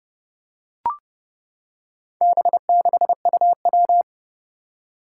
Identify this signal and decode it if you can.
Morse code, heard as a keyed tone, about 30 wpm, 700 Hz. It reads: B6UW
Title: Four Challenging Letter-Number Combinations 30wpm